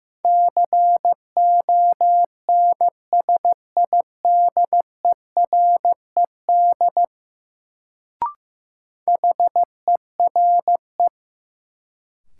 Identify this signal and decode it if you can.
Morse code, heard as a keyed tone, about 15 wpm, 700 Hz. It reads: CONSIDERED HERE
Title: Sets of 5 Words - New Word Drill - 15wpm